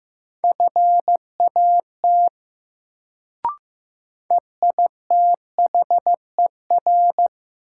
Morse code, heard as a keyed tone, about 15 wpm, 700 Hz. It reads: FAT EITHER
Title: Sentences from Top 600 Words - New Word Drill - 15wpm